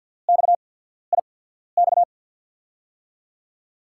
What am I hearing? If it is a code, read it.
Morse code, heard as a keyed tone, about 49 wpm, spaced as for 14 wpm, 700 Hz. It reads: XIX
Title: USA Call Signs - 5 Characters - 8x Character Spacing - 50wpm